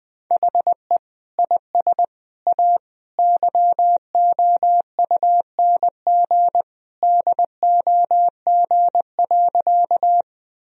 Morse code, heard as a keyed tone, about 20 wpm, 700 Hz. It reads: HE IS A YOUNG DOG.
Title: Sentences from Top 1000 Words - Review Part 2 - 20wpm